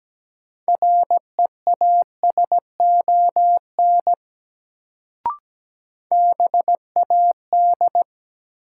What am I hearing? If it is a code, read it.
Morse code, heard as a keyed tone, about 17 wpm, 700 Hz. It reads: REASON BAD
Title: Sentences from Top 700 Words - New Word Drill - 17wpm